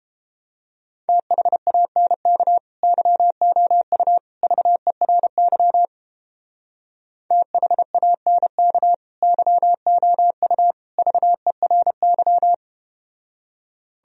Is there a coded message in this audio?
Yes: Morse code from 1.1 to 12.5 s, heard as a keyed tone, about 32 wpm, 700 Hz. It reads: THANK YOU VERY THANK YOU VERY